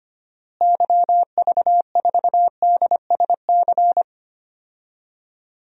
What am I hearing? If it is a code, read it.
Morse code, heard as a keyed tone, about 25 wpm, 700 Hz. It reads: YV4DSC
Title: DXCC-100 Call Signs - 6 Characters - Repeated 2x - 25wpm